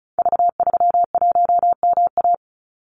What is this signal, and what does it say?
Morse code, heard as a keyed tone, about 36 wpm, 700 Hz. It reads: V31MA
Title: Call Signs - 35wpm